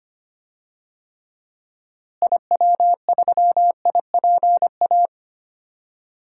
Morse code, heard as a keyed tone, about 25 wpm, 700 Hz. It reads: IW3IPA
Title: DXCC-100 Call Signs - 6 Characters - Repeated 3x - 25wpm